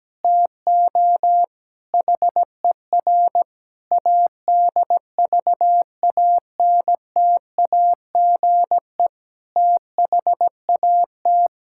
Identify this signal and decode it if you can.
Morse code, heard as a keyed tone, about 17 wpm, 700 Hz. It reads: TO HER ADVANTAGE THAT
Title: Sentences from Top 1000 Words - Review Part 6 - 17wpm